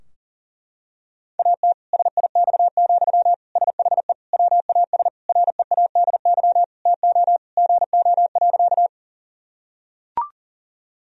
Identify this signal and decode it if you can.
Morse code, heard as a keyed tone, about 40 wpm, 700 Hz. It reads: AT SIX, SHE WAS READY TO GO.
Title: Sentences from Top 1000 Words - Review Part 2 - 40wpm